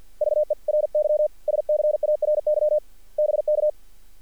Morse code, heard as a keyed tone, about 46 wpm, 590 Hz. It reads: VERY SCARY BK
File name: archived / CRSnet